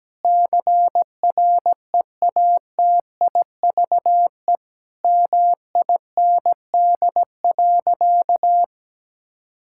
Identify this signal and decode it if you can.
Morse code, heard as a keyed tone, about 17 wpm, 700 Hz. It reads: CREATIVE MIND.
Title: Sentences from Top 1000 Words - Review Part 5 - 17wpm